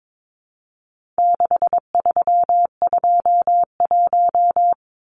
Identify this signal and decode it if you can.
Morse code, heard as a keyed tone, about 22 wpm, 700 Hz. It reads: 6321
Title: Numbers - 4-Digit 22wpm